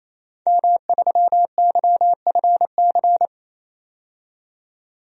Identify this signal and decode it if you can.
Morse code, heard as a keyed tone, about 28 wpm, 700 Hz. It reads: M3YFC
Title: DXCC-100 Call Signs - 5 Characters - 28wpm